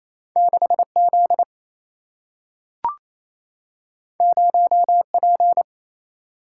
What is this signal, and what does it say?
Morse code, heard as a keyed tone, about 28 wpm, 700 Hz. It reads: 6 Z 0 P
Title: Single-Letter Single-Number 28wpm